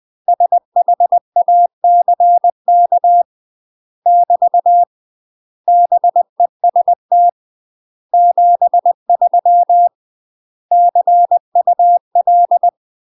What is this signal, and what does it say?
Morse code, heard as a keyed tone, about 20 wpm, 700 Hz. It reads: SHACK <BT> BEST 73 CUL
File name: UK MASTER QSOs TRAINING / 20WPM / UK MASTER QSOs 0049